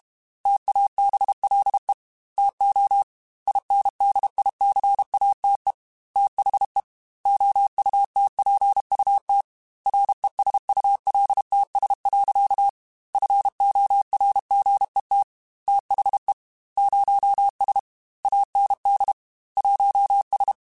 Morse code, heard as a keyed tone, about 32 wpm, 800 Hz. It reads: TABLE TO INDICATE THE OUTPUT RESULTS. FORGET THE 0S AND 1S
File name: Technical Subjects / Technical Subjects-mp3s / Gates Intro @30